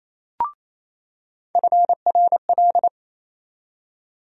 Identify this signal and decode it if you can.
Morse code, heard as a keyed tone, about 28 wpm, 700 Hz. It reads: FRL
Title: USA Call Sign Postfixes - 3 Letters 28wpm